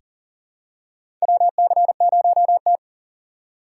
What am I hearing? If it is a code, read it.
Morse code, heard as a keyed tone, about 37 wpm, 700 Hz. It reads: WC0T
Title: USA Call Signs - 3x Character Spacing - 40wpm